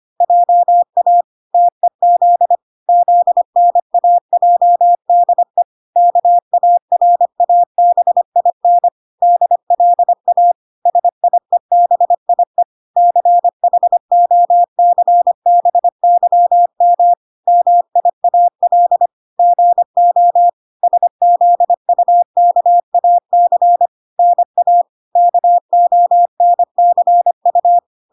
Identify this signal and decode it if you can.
Morse code, heard as a keyed tone, about 25 wpm, 700 Hz. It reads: JA TEZ ZNAJDE KARABIN DLA SIEBIE CHOCBYM MIAL GO SZUKAC NA KONCU